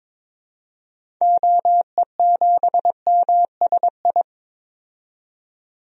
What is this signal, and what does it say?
Morse code, heard as a keyed tone, about 22 wpm, 700 Hz. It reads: OE7MSI
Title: DXCC-100 Call Signs - 6 Characters - 22wpm